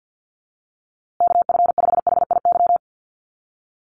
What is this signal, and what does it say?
Morse code, heard as a keyed tone, about 50 wpm, 700 Hz. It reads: KF5HIY
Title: USA Call Signs Repeated 3x 50wpm